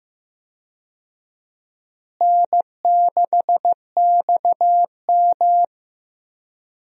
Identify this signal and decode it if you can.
Morse code, heard as a keyed tone, about 15 wpm, 700 Hz. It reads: N6XM
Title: USA Call Signs - 4 Characters - Repeated 2x - 15wpm